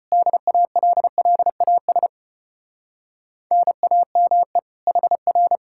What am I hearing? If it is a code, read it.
Morse code, heard as a keyed tone, about 33 wpm, 700 Hz. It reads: DALLAS NAME HR